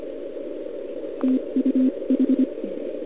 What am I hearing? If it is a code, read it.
Morse code, heard as a keyed tone, about 25 wpm, spaced as for 22 wpm, 260 Hz. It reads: TUH
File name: ARCHIV / OKDX2020 / OL4Wa